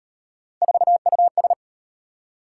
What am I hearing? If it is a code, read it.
Morse code, heard as a keyed tone, about 38 wpm, 700 Hz. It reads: 4US